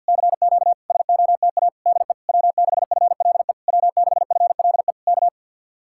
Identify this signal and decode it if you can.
Morse code, heard as a keyed tone, about 66 wpm, spaced as for 58 wpm, 700 Hz. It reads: CQ SOTA DE W6FLE W6FLE K
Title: Speed-Racing Common QSO Phrases - 50wpm